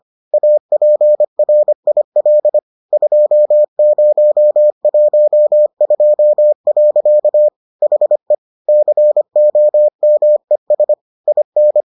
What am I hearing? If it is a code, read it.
Morse code, heard as a keyed tone, about 25 wpm, 600 Hz. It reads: APRIL 2012. HE COMES IN